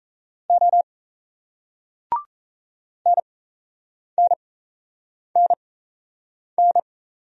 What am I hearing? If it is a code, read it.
Morse code, heard as a keyed tone, about 38 wpm, spaced as for 8 wpm, 700 Hz. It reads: O NNNN